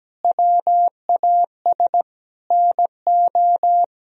Morse code, heard as a keyed tone, about 17 wpm, 700 Hz. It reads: WAS NO
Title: Sentences from Top 1000 Words - Review Part 3 - 17wpm